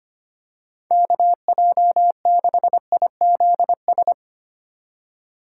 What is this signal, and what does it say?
Morse code, heard as a keyed tone, about 25 wpm, 700 Hz. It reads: KJ6IZS
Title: USA Call Signs - 4x Character Spacing - 25wpm